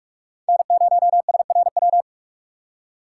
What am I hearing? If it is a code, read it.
Morse code, heard as a keyed tone, about 46 wpm, 700 Hz. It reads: N0SRW
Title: USA Call Signs - 5 Characters - 4x Character Spacing - 45wpm